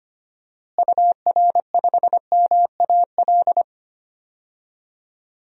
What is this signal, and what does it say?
Morse code, heard as a keyed tone, about 25 wpm, 700 Hz. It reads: UR5MAL
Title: DXCC-100 Call Signs - 6 Characters - Repeated 3x - 25wpm